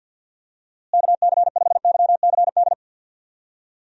Morse code, heard as a keyed tone, about 48 wpm, 700 Hz. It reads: KC5YCD
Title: USA Call Signs - 6 Characters - Repeated 2x - 50wpm